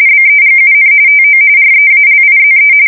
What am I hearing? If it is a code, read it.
RTTY, about 45.45 baud. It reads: BEGUN TO REALIZE